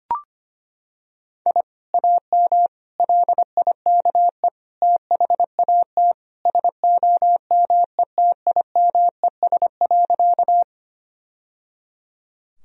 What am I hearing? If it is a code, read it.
Morse code, heard as a keyed tone, about 25 wpm, 700 Hz. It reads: I AM LIKE THAT SOMETIMES.
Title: Sentences from Top 1000 Words - Review Part 3 - 25wpm